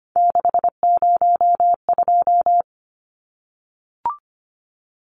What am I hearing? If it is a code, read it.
Morse code, heard as a keyed tone, about 25 wpm, 700 Hz. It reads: 602
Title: Numbers - 3-Digit 25wpm